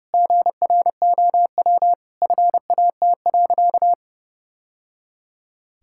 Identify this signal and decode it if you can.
Morse code, heard as a keyed tone, about 30 wpm, 700 Hz. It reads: GROW FAT.
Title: Sentences from Top 1000 Words - Review Part 5 - 30wpm